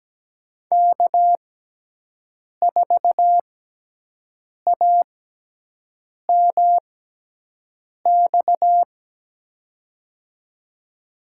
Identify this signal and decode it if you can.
Morse code, heard as a keyed tone, about 17 wpm, spaced as for 6 wpm, 700 Hz. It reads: K4AMX
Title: USA Call Signs - 5 Characters - 6x Character Spacing - 17wpm